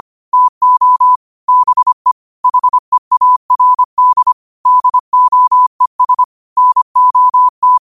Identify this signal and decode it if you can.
Morse code, heard as a keyed tone, about 25 wpm, 1000 Hz. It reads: TO BE HEARD DOES NOT
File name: qotdfiles / 25wpm